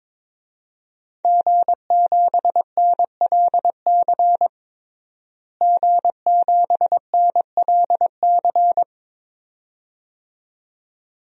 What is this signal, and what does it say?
Morse code, heard as a keyed tone, about 21 wpm, 700 Hz. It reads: G7NLC G7NLC
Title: DXCC-100 Call Signs - 5 Characters - Repeated 2x - 22wpm